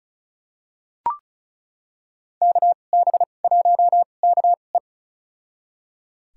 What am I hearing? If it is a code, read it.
Morse code, heard as a keyed tone, about 34 wpm, spaced as for 25 wpm, 700 Hz. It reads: KB1KE
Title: USA Call Signs - 5 Characters - 2x Character Spacing - 35wpm